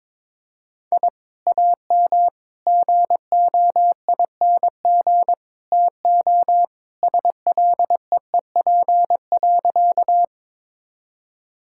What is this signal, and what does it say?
Morse code, heard as a keyed tone, about 22 wpm, 700 Hz. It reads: I AM GOING TO SLEEP.
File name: practice / 22wpm / Sentences with 2-Word Phrases - 22wpm